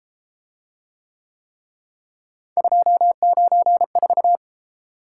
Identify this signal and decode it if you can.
Morse code, heard as a keyed tone, about 29 wpm, 700 Hz. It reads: 294